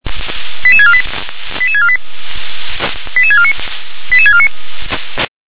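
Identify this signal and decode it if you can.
Five-tone selective call (ZVEI), heard as quick runs of five tones, tones of about 70 ms. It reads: 88648 88648 88649 88649
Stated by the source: ZVEI five-tone selective call received with an SDR.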